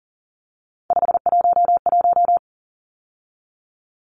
Morse code, heard as a keyed tone, about 40 wpm, 700 Hz. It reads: 511
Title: Numbers - 3-Digit 40wpm